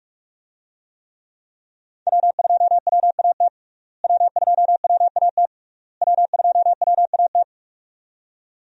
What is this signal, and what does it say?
Morse code, heard as a keyed tone, about 46 wpm, 700 Hz. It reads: W2WAT W2WAT W2WAT
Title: USA Call Signs - 5 Characters - Repeated 3x - 45wpm